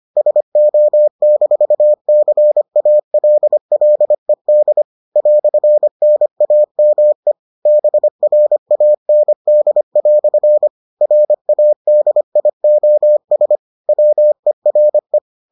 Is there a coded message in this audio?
Yes: Morse code from 0.2 to 15.2 s, heard as a keyed tone, about 25 wpm, 600 Hz. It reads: SO-CALLED "NAME BRAND" RADIOS WERE